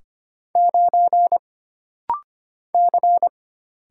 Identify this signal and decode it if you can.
Morse code, heard as a keyed tone, about 25 wpm, 700 Hz. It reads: 9 C
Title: Single Letter-Number - Rapid-Fire 25wpm